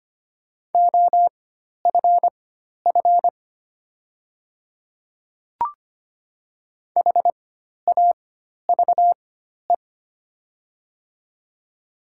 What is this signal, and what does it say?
Morse code, heard as a keyed tone, about 25 wpm, spaced as for 12 wpm, 700 Hz. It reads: OFF HAVE
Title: 4x Character Spacing - Top 200 Words - 25wpm